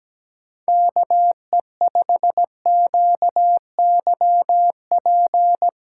Morse code, heard as a keyed tone, about 17 wpm, 700 Hz. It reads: KE5QYP
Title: USA Callsign DE Callsign - 6 Characters - 17wpm